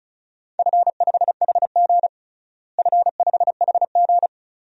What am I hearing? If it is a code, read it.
Morse code, heard as a keyed tone, about 34 wpm, 700 Hz. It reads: F5HG F5HG
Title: DXCC-100 Call Signs - 4 Characters - Repeated 2x - 35wpm